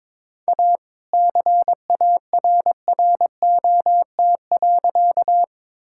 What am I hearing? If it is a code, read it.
Morse code, heard as a keyed tone, about 23 wpm, 700 Hz. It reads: A CARROT.
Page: morse-code-ninja-practice-22wpm